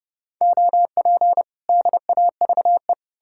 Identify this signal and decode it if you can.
Morse code, heard as a keyed tone, about 41 wpm, spaced as for 36 wpm, 700 Hz. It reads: OP DAVE
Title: Speed-Racing Common QSO Phrases Encore - 30wpm